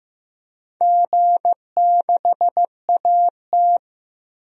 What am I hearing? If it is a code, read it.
Morse code, heard as a keyed tone, about 15 wpm, 700 Hz. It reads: G6AT
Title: DXCC-100 Call Signs - 4 Characters - 15wpm